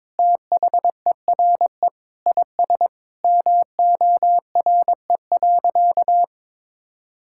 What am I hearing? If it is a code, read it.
Morse code, heard as a keyed tone, about 22 wpm, 700 Hz. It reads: THERE IS MORE.
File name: practice / 22wpm / Sentences with 3-Word Phrases - 22wpm